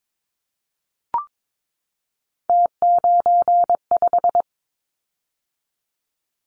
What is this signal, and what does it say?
Morse code, heard as a keyed tone, about 22 wpm, 700 Hz. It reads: T95
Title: Callsign Prefixes for Top 100 DXCC - 22wpm